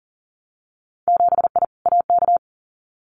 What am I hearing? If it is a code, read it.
Morse code, heard as a keyed tone, about 41 wpm, 700 Hz. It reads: 7I AK
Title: Winter Field Day Exchanges Encore 40wpm